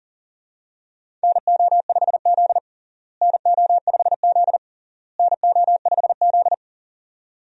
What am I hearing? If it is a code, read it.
Morse code, heard as a keyed tone, about 40 wpm, 700 Hz. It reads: NO5Z NO5Z NO5Z
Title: USA Call Signs - 4 Characters - Repeated 3x - 40wpm